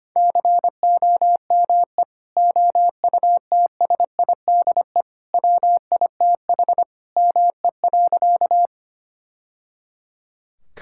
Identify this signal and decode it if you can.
Morse code, heard as a keyed tone, about 25 wpm, 700 Hz. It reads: COME OUTSIDE WITH ME.
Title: Sentences from Top 1000 Words - Review Part 6 - 25wpm